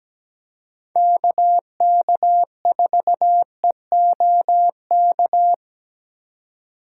Morse code, heard as a keyed tone, about 17 wpm, 700 Hz. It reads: KK4EOK
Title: USA Call Signs - 6 Characters - 3x Character Spacing - 17wpm